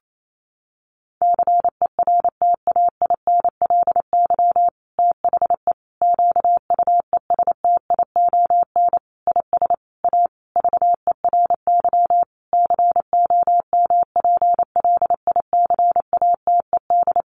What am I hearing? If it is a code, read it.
Morse code, heard as a keyed tone, about 28 wpm, 700 Hz. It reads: CERTAINLY THE QUESTION IS A VERY COMPLICATED
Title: Sentences from Top 1200 Words 28wpm